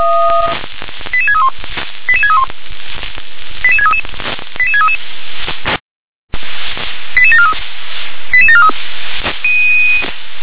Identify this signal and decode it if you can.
Five-tone selective call (ZVEI), heard as quick runs of five tones, tones of about 70 ms. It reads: 88631 88631 88633 88633 88743 88743
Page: SDR/ZVEI